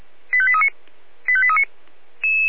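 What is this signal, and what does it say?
Five-tone selective call (ZVEI), heard as quick runs of five tones, tones of about 70 ms. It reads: 85839 85839
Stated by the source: ZVEI five-tone selective call received on SDR.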